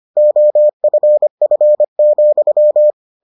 Morse code, heard as a keyed tone, about 25 wpm, 600 Hz. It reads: OFF,